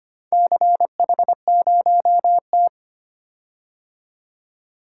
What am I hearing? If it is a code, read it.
Morse code, heard as a keyed tone, about 25 wpm, 700 Hz. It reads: CH0T
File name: practice / 25wpm / Four Letter-Number Combinations - 25wpm